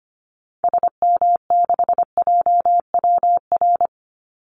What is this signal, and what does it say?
Morse code, heard as a keyed tone, about 25 wpm, 700 Hz. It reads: SM6JWR
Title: Call Signs Repeated 3x - 25wpm